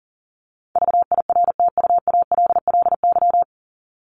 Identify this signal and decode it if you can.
Morse code, heard as a keyed tone, about 40 wpm, 700 Hz. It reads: VIRTUALLY
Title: LY Suffix - Top 100 Words with Base Words 40wpm